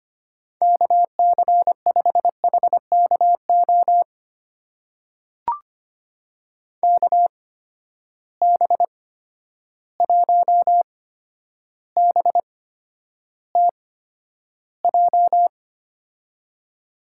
Morse code, heard as a keyed tone, about 25 wpm, spaced as for 11 wpm, 700 Hz. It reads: KC5HKO KB1BTJ
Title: USA Call Signs - 6 Characters - 8x Character Spacing - 25wpm